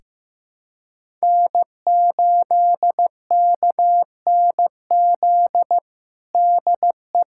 Morse code, heard as a keyed tone, about 15 wpm, 700 Hz. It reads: N8KNZ DE
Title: USA Callsign DE Callsign - 5 Characters - 15wpm